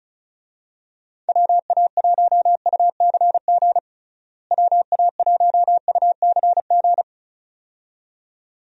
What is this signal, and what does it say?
Morse code, heard as a keyed tone, about 36 wpm, 700 Hz. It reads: WA1UCG WA1UCG
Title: USA Call Signs - 6 Characters - Repeated 2x - 35wpm